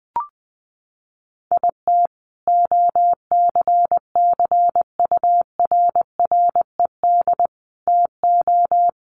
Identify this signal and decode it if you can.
Morse code, heard as a keyed tone, about 20 wpm, 700 Hz. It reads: IT OCCURRED TO
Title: Sets of 5 Words Encore 20wpm